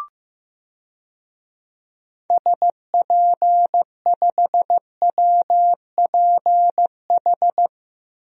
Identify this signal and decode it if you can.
Morse code, heard as a keyed tone, about 15 wpm, 700 Hz. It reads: SP5WPH
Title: DXCC-100 Call Signs - 6 Characters - Repeated 3x - 15wpm